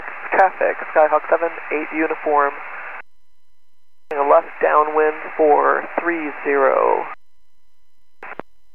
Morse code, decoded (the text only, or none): none